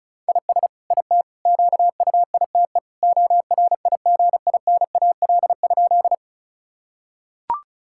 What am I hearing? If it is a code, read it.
Morse code, heard as a keyed tone, about 36 wpm, 700 Hz. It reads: IS IT QUITE ORIGINAL?
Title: Sentences from Top 1000 Words - Review Part 6 - 35wpm